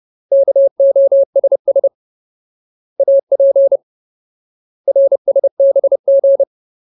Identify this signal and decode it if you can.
Morse code, heard as a keyed tone, about 30 wpm, 550 Hz. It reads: KOSS AP RSBG
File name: Ultimate Word List / 30WPM Ultimate